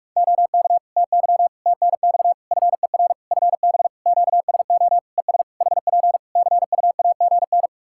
Morse code, heard as a keyed tone, about 46 wpm, 700 Hz. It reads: OK TY TNX FER FB QSO ES HP CUAGN
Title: K1USN SST QSO Practice 45wpm